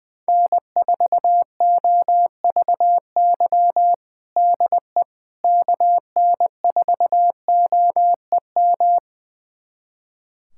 Morse code, heard as a keyed tone, about 20 wpm, 700 Hz. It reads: N4OVY DE KN4OEM
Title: USA Callsign DE Callsign 20wpm